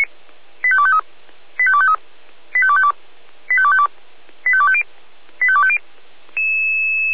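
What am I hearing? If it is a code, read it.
Five-tone selective call (ZVEI), heard as quick runs of five tones, tones of about 70 ms. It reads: 85253 85253 85252 85252 85250 85250
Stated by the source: ZVEI five-tone selective call 85230, received on SDR by feuerwehr-orlamuende fire brigade (F85360).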